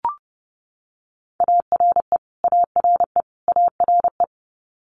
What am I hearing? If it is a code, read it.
Morse code, heard as a keyed tone, about 30 wpm, 700 Hz. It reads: ARE ARE ARE
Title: Multi-letter Phonemes Repeated 3x 30wpm